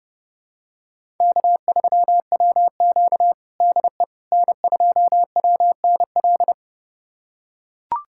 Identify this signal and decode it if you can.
Morse code, heard as a keyed tone, about 32 wpm, 700 Hz. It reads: K3WQ DE N2WNL
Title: USA Callsign DE Callsign 30wpm